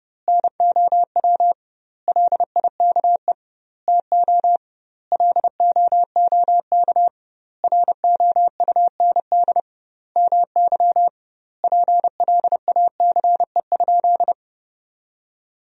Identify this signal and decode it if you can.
Morse code, heard as a keyed tone, about 30 wpm, 700 Hz. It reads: NOW LIKE TO LOOK ROUND MY PLACE?
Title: 2x Word-Spacing - Sentences from Top 200 Words - 30wpm